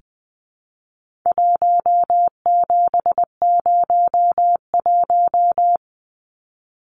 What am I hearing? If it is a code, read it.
Morse code, heard as a keyed tone, about 20 wpm, 700 Hz. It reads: 1701
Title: Numbers - 4-Digit 20wpm